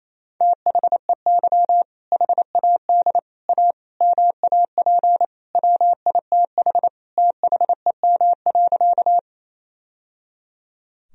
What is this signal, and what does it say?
Morse code, heard as a keyed tone, about 28 wpm, 700 Hz. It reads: THEY HAD A MAP WITH THEM.
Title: Sentences from Top 1000 Words - Review Part 2 - 28wpm